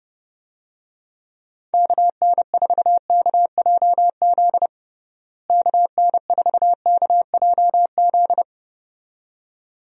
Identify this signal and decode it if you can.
Morse code, heard as a keyed tone, about 30 wpm, 700 Hz. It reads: KN4KJZ KN4KJZ
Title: USA Call Signs - 6 Characters - Repeated 2x - 30wpm